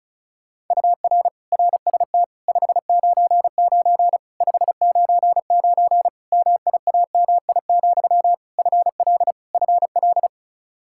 Morse code, heard as a keyed tone, about 36 wpm, 700 Hz. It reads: UR RST 599 599 MIAMI, FL FL
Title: Practice QSO - Exchange 1 - Beginning - 35wpm